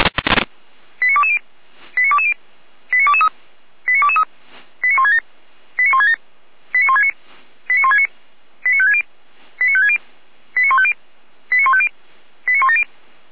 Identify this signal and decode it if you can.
Five-tone selective call (ZVEI), heard as quick runs of five tones, tones of about 70 ms. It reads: 89220 89220 89223 89223 89167 89167 89169 89169 89560 89560 89150 89150 89170